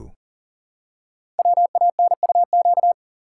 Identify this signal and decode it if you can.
Morse code, heard as a keyed tone, about 42 wpm, 700 Hz. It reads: WANUQ
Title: Code Groups - 5 Letter:Numbers - 40wpm